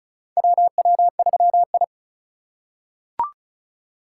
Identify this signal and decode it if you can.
Morse code, heard as a keyed tone, about 34 wpm, 700 Hz. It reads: WW3I
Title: USA Call Signs - 4 Characters - 3x Character Spacing - 35wpm